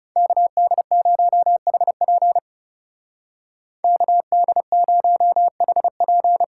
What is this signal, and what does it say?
Morse code, heard as a keyed tone, about 37 wpm, 700 Hz. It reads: KD0HP KD0HP
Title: Speed Racing - USA Call Signs 30wpm